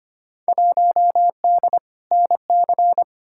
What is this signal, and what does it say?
Morse code, heard as a keyed tone, about 25 wpm, 700 Hz. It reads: 1D NC
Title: Field Day Exchanges - 25wpm